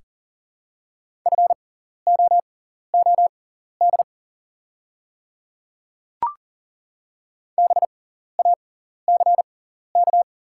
Morse code, heard as a keyed tone, about 40 wpm, spaced as for 14 wpm, 700 Hz. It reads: FOOD BACK
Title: 6x Character Spacing - Top 300 Words - 40wpm